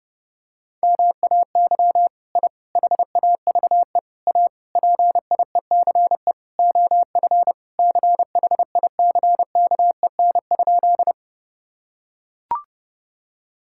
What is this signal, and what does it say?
Morse code, heard as a keyed tone, about 32 wpm, 700 Hz. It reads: MAY I HAVE A PIECE OF CHICKEN?
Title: Sentences from Top 1000 Words - Review Part 5 - 30wpm